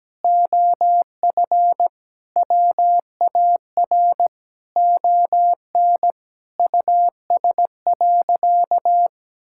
Morse code, heard as a keyed tone, about 17 wpm, 700 Hz. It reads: OF WAR ON US.
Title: Sentences from Top 1000 Words - Review Part 6 - 17wpm